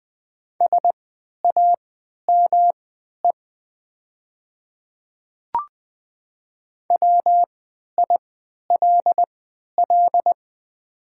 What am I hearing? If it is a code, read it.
Morse code, heard as a keyed tone, about 20 wpm, spaced as for 11 wpm, 700 Hz. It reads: SAME WILL
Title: 3x Character Spacing - Top 300 Words - 20wpm